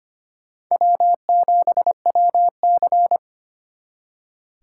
Morse code, heard as a keyed tone, about 32 wpm, 700 Hz. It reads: W7WC